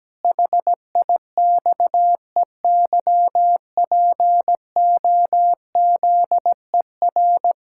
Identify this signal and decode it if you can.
Morse code, heard as a keyed tone, about 17 wpm, 700 Hz. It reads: HIXEYPOZER
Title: Pangram Challenge v2 - 17wpm